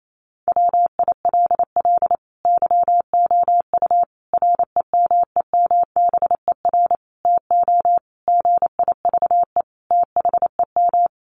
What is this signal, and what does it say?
Morse code, heard as a keyed tone, about 28 wpm, 700 Hz. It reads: WILL YOU REMEMBER TO GIVE THEM
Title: Sentences from Top 1100 Words 28wpm